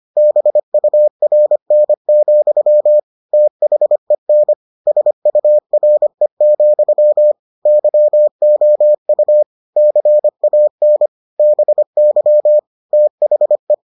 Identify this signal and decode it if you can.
Morse code, heard as a keyed tone, about 25 wpm, 600 Hz. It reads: BURN, THEN SURE, YOU CAN BY THE